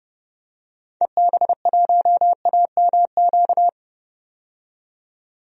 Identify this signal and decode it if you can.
Morse code, heard as a keyed tone, about 30 wpm, 700 Hz. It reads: EB1AMQ
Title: DXCC-100 Call Signs - 6 Characters - 30wpm